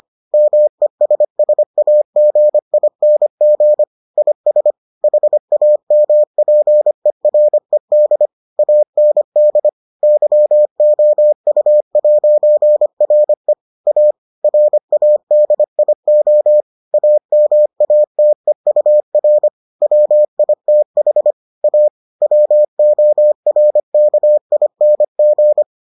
Morse code, heard as a keyed tone, about 25 wpm, 600 Hz. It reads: MESSAGING IS HAMPERED AND YOU'RE A RADIO AMATEUR WITH A WORKING